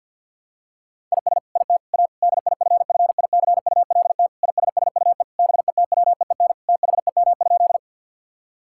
Morse code, heard as a keyed tone, about 52 wpm, 700 Hz. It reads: IS IT A DIFFICULT ISSUE BETWEEN THEM?
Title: Sentences from Top 1000 Words - Review Part 6 - 50wpm